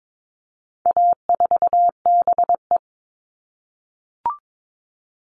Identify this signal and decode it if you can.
Morse code, heard as a keyed tone, about 22 wpm, 700 Hz. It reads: A4BE
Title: All Cut Numbers - 4-Digit 22wpm